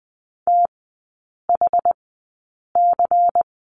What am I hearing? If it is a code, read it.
Morse code, heard as a keyed tone, about 20 wpm, spaced as for 8 wpm, 700 Hz. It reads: THC
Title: 093 Lesson 16e - Letter c - Three Characters 20wpm